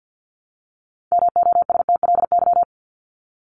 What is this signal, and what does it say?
Morse code, heard as a keyed tone, about 50 wpm, 700 Hz. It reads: MOSTLY
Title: Top 1001-1250 Words 50wpm